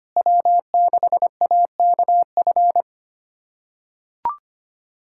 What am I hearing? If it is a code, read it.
Morse code, heard as a keyed tone, about 25 wpm, 700 Hz. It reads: W6AKF
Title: USA Callsign DE Callsign - 5 Characters - 25wpm